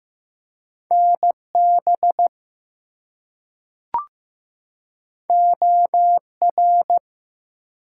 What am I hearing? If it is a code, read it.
Morse code, heard as a keyed tone, about 15 wpm, 700 Hz. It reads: NB OR
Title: US States and Canadian Provinces 15wpm